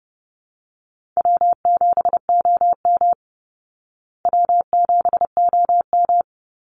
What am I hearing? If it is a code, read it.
Morse code, heard as a keyed tone, about 30 wpm, 700 Hz. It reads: W7OM W7OM
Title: Easy Call Signs Repeated 2x - 30wpm